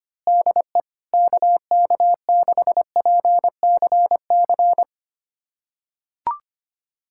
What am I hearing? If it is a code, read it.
Morse code, heard as a keyed tone, about 25 wpm, 700 Hz. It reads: DE KK6PCC
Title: USA Callsign DE Callsign 25wpm